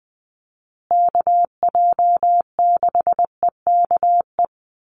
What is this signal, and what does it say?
Morse code, heard as a keyed tone, about 20 wpm, 700 Hz. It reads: KJ6EKE
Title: USA Call Signs Repeated 2x 20wpm